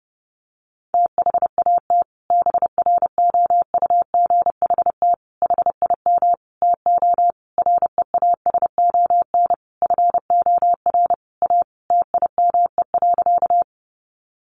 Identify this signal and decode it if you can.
Morse code, heard as a keyed tone, about 30 wpm, 700 Hz. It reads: THAT BROUGHT HIM TO REASON FOR A TIME.
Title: Sentences from Top 700 Words 30wpm